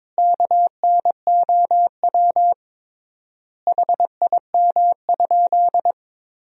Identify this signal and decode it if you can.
Morse code, heard as a keyed tone, about 21 wpm, 700 Hz. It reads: KNOW HIM?
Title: 3x Word-Spacing - Sentences from Top 100 Words - 22wpm